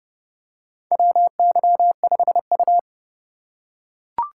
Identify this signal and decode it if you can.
Morse code, heard as a keyed tone, about 30 wpm, 700 Hz. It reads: WY5U
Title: USA Call Signs - 2x Character Spacing - 30wpm